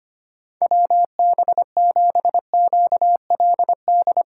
Morse code, heard as a keyed tone, about 25 wpm, 700 Hz. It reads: WB7QLD
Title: USA Call Signs - 4x Character Spacing - 25wpm